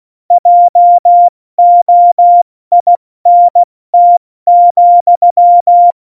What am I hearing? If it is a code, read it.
Morse code, heard as a keyed tone, about 16 wpm, 700 Hz. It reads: JOINT,
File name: Morsekurs / Witze / Witz 08. Dezember 2020, 70 F80